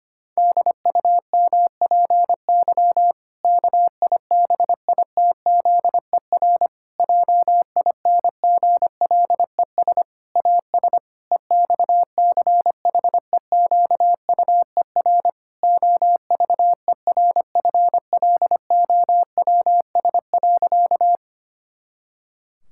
Morse code, heard as a keyed tone, about 25 wpm, 700 Hz. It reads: DUMPY KIBITZER JINGLES AS EXCHEQUER OVERFLOWS.
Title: Pangram Challenge v2 - 25wpm